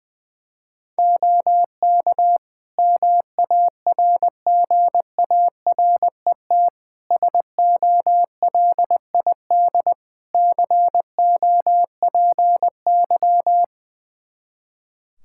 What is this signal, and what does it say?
Morse code, heard as a keyed tone, about 20 wpm, 700 Hz. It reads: OK MARGARET SOLID COPY
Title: Practice QSO - Exchange 3 - More Info - 20wpm